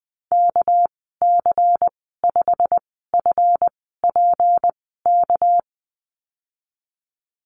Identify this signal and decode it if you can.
Morse code, heard as a keyed tone, about 20 wpm, 700 Hz. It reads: K C 5 F P K
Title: 238 Lesson 38g - Letter q - Call Signs 20wpm